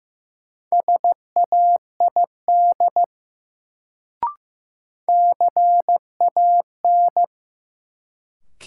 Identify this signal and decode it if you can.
Morse code, heard as a keyed tone, about 15 wpm, 700 Hz. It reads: SAID CAN
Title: Sentences from Top 100 Words - New Word Drill - 15wpm